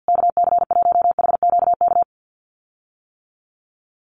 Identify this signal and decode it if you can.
Morse code, heard as a keyed tone, about 48 wpm, 700 Hz. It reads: KC1HQK
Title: USA Call Signs Repeated 3x 50wpm